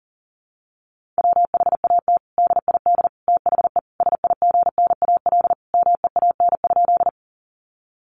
Morse code, heard as a keyed tone, about 40 wpm, 700 Hz. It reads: WHAT DID THE SIGNAL MEAN?
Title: Sentences from Top 1400 Words 40wpm